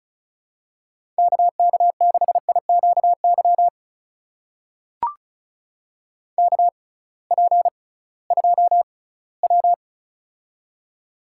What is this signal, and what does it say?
Morse code, heard as a keyed tone, about 36 wpm, spaced as for 20 wpm, 700 Hz. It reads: KK6IQY KP2W
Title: USA Call Signs - 6x Character Spacing - 35wpm